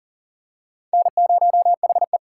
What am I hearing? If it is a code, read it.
Morse code, heard as a keyed tone, about 40 wpm, 700 Hz. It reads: N0HE
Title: USA Call Signs - 4 Characters - 6x Character Spacing - 40wpm